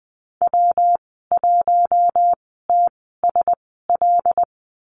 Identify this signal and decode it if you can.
Morse code, heard as a keyed tone, about 20 wpm, spaced as for 15 wpm, 700 Hz. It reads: W1TSL
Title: 128 Lesson 21g - Number 6 - Call Signs 20wpm